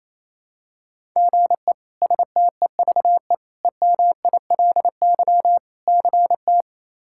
Morse code, heard as a keyed tone, about 28 wpm, 700 Hz. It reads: GE STEVE EMILY CT
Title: K1USN SST QSO Practice - 28wpm